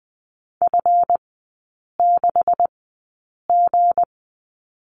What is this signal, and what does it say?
Morse code, heard as a keyed tone, about 20 wpm, spaced as for 8 wpm, 690 Hz. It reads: F6G
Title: 198 Lesson 32e - Letter k - Three Characters 20wpm